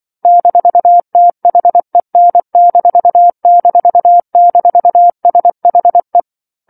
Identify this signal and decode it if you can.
Morse code, heard as a keyed tone, about 24 wpm, 700 Hz. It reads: -THEN---SHE
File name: 07-Chapter-07 24 WPM